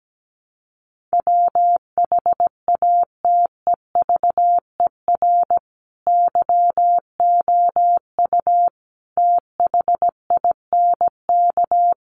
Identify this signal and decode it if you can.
Morse code, heard as a keyed tone, about 17 wpm, 700 Hz. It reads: WHATEVER YOU THINK
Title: Sentences from Top 1100 Words 17wpm